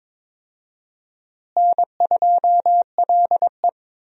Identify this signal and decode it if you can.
Morse code, heard as a keyed tone, about 22 wpm, 700 Hz. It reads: N2LE
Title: USA Call Signs - 4 Characters - Repeated 3x - 22wpm